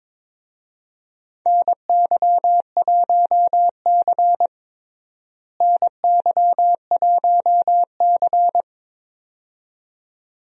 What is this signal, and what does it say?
Morse code, heard as a keyed tone, about 22 wpm, 700 Hz. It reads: NY1C NY1C
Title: USA Call Signs - 4 Characters - Repeated 2x - 22wpm